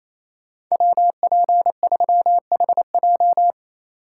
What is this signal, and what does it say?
Morse code, heard as a keyed tone, about 37 wpm, 700 Hz. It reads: WP3HJ